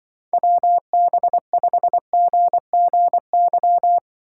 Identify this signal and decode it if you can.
Morse code, heard as a keyed tone, about 23 wpm, 700 Hz. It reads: WB5GGY
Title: Speed Racing - USA 2x3 Call Signs 20wpm